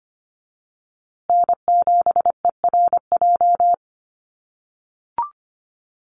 Morse code, heard as a keyed tone, about 25 wpm, 700 Hz. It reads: N7ERJ
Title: USA Call Signs Repeated 2x 25wpm